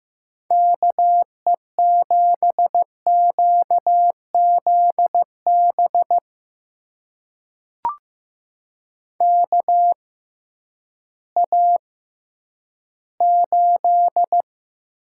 Morse code, heard as a keyed tone, about 15 wpm, spaced as for 10 wpm, 700 Hz. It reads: KE7QZB KA8
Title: USA Call Signs - 6 Characters - 6x Character Spacing - 15wpm